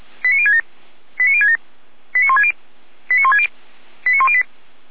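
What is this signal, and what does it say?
Five-tone selective call (ZVEI), heard as quick runs of five tones, tones of about 70 ms. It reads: 89067 89067 89160 89160 89108
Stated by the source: ZVEI five-tone selective call received with an SDR.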